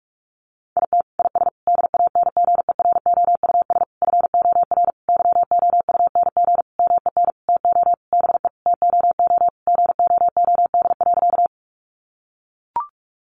Morse code, heard as a keyed tone, about 44 wpm, 700 Hz. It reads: IT IS DANGEROUS FOR YOUNG MEN TO BE TOO GOOD.
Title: Sentences from Top 1200 Words 45wpm